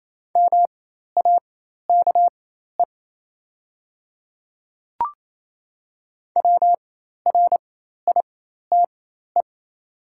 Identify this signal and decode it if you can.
Morse code, heard as a keyed tone, about 28 wpm, spaced as for 13 wpm, 690 Hz. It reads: MAKE WRITE
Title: 4x Character Spacing - Top 100 Words - 28wpm